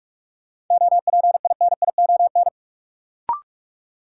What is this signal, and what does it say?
Morse code, heard as a keyed tone, about 42 wpm, 700 Hz. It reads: OPINION
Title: Sentences from Top 1200 Words - New Word Drill - 45wpm